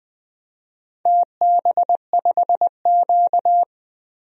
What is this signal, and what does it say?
Morse code, heard as a keyed tone, about 19 wpm, 700 Hz. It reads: TB5Q
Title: DXCC-100 Call Signs - 4 Characters - Repeated 3x - 20wpm